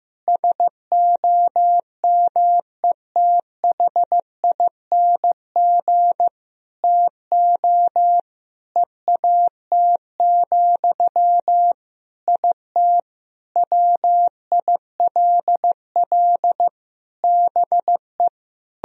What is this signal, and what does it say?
Morse code, heard as a keyed tone, about 15 wpm, 700 Hz. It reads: SOMETHING TO EAT, IT WILL BE
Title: Sentences from Top 1000 Words - Review Part 1 - 15wpm